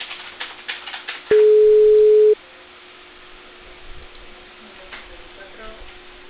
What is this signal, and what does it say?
Ringback tone (Europe, 425 Hz), with speech.